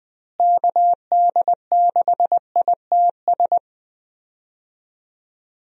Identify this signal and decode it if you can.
Morse code, heard as a keyed tone, about 20 wpm, 700 Hz. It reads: KD6ITS
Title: USA Call Signs - 6 Characters - Repeated 2x - 20wpm